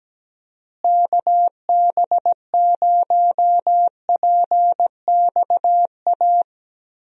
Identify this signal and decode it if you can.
Morse code, heard as a keyed tone, about 17 wpm, 700 Hz. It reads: KB0PXA
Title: USA Call Signs - 6 Characters - Repeated 3x - 17wpm